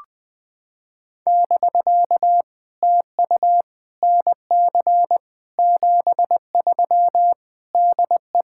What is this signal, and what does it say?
Morse code, heard as a keyed tone, about 20 wpm, 700 Hz. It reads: <BK> TU NC 73 DE
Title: POTA QSO Practice - 20wpm